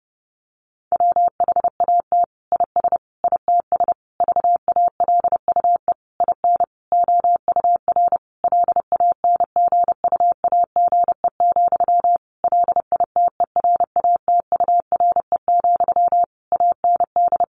Morse code, heard as a keyed tone, about 28 wpm, 700 Hz. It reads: WHAT IS ITS VALUE IN OUR LANGUAGE, LITERATURE, AND
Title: Sentences from Top 600 Words 30wpm